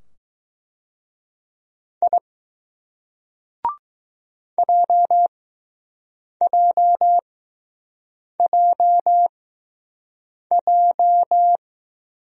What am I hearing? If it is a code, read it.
Morse code, heard as a keyed tone, about 20 wpm, spaced as for 6 wpm, 700 Hz. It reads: I JJJJ